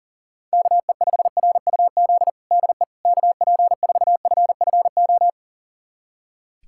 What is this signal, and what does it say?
Morse code, heard as a keyed tone, about 40 wpm, 700 Hz. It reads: KE5RUZ DE KP4FFO
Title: USA Callsign DE Callsign - 6 Characters - 40wpm